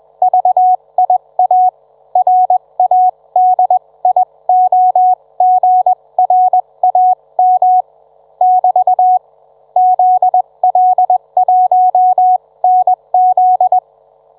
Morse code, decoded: VIA RADIOGRAM <BT> ZL1NZ